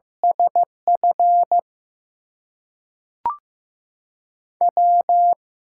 Morse code, heard as a keyed tone, about 15 wpm, 700 Hz. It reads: SF W
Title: USA Call Signs - 8x Character Spacing - 15wpm